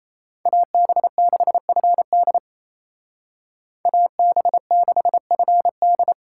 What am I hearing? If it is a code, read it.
Morse code, heard as a keyed tone, about 36 wpm, 700 Hz. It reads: AB6FD AB6FD